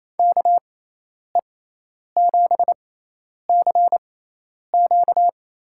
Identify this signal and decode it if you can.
Morse code, heard as a keyed tone, about 28 wpm, spaced as for 10 wpm, 700 Hz. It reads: KE7CQ
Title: USA Call Signs - 6 Characters - 6x Character Spacing - 28wpm